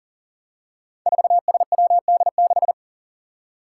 Morse code, heard as a keyed tone, about 39 wpm, 700 Hz. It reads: 4SWD6
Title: Five Challenging Letter-Number Combinations 40wpm